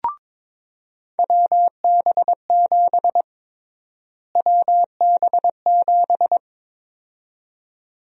Morse code, heard as a keyed tone, about 22 wpm, 700 Hz. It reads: WB7 WB7
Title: USA Call Sign Prefixes - Repeated 2x - 22wpm